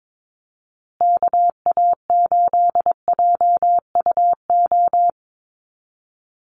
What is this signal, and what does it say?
Morse code, heard as a keyed tone, about 22 wpm, 700 Hz. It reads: KA8JUO
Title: USA Call Signs Repeated 2x 22wpm